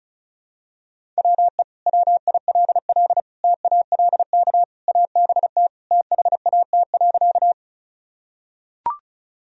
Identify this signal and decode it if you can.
Morse code, heard as a keyed tone, about 36 wpm, 700 Hz. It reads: WE WILL TALK ABT THAT.
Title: Sentences from Top 400 Words and Abbreviations 35wpm